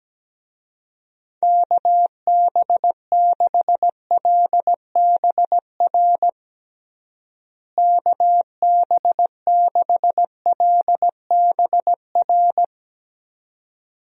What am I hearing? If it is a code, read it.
Morse code, heard as a keyed tone, about 17 wpm, 700 Hz. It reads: KB6LBR KB6LBR
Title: USA Call Signs - 6 Characters - Repeated 2x - 17wpm